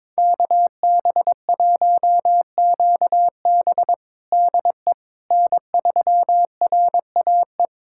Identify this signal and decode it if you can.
Morse code, heard as a keyed tone, about 22 wpm, 700 Hz. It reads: KB1QB DE N3RAE
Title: USA Callsign DE Callsign - 5 Characters - 22wpm